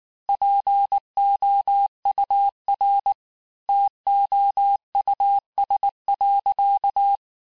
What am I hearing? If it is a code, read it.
Morse code, heard as a keyed tone, about 19 wpm, 800 Hz. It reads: POUR TOUS.
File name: cours-du-lundi files / cours-1583-fin-QSO-18-mpm